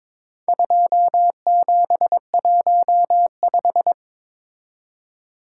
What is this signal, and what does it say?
Morse code, heard as a keyed tone, about 23 wpm, 700 Hz. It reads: 2715
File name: practice / 22wpm / Four Letter-Number Combinations - 22wpm